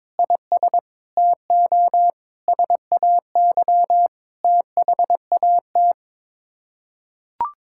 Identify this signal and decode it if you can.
Morse code, heard as a keyed tone, about 22 wpm, 700 Hz. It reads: IS TO SAY THAT
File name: practice / 22wpm / Sets of 7 Words - 22wpm